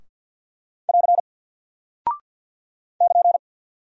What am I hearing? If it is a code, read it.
Morse code, heard as a keyed tone, about 48 wpm, 700 Hz. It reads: <AR> <KN>
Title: Single Character (Letter:Number:Punctuation:Prosign) Alt 50wpm